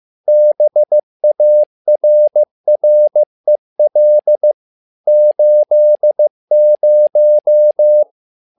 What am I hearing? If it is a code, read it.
Morse code, heard as a keyed tone, about 15 wpm, 600 Hz. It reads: BARREL 80
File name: learning center / tower trouble / tower trouble 15wpm